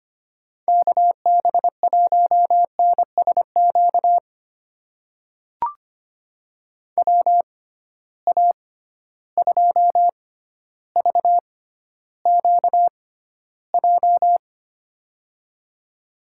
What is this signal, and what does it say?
Morse code, heard as a keyed tone, about 25 wpm, spaced as for 13 wpm, 700 Hz. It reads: KB1NSQ WA2VQJ
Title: USA Call Signs - 6 Characters - 6x Character Spacing - 25wpm